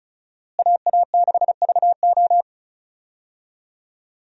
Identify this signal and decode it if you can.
Morse code, heard as a keyed tone, about 34 wpm, 700 Hz. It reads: AA6VO
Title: USA Call Signs - 5 Characters - Repeated 2x - 35wpm